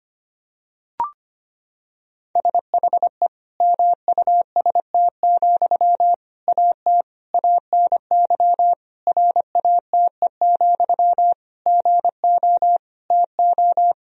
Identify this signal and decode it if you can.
Morse code, heard as a keyed tone, about 25 wpm, 700 Hz. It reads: SHE MUST, AT ANY RATE, GO TO
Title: Sentences from Top 1000 Words - Review Part 4 - 25wpm